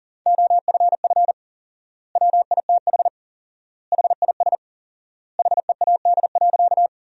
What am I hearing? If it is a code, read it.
Morse code, heard as a keyed tone, about 40 wpm, 700 Hz. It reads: OFF WITH HIS HEAD.
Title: 4x Word-Spacing - Sentences from Top 300 Words - 40wpm